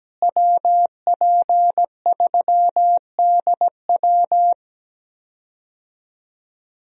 Morse code, heard as a keyed tone, about 17 wpm, 700 Hz. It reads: WP3DW
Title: USA Call Signs - 5 Characters - Repeated 2x - 17wpm